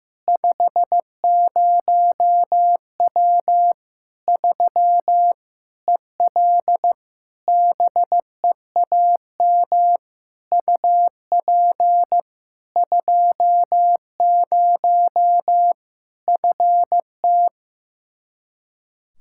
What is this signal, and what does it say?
Morse code, heard as a keyed tone, about 15 wpm, 700 Hz. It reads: 50W 3 EL BEAM UP 20 FT
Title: Practice QSO - Exchange 2 - Info - 15wpm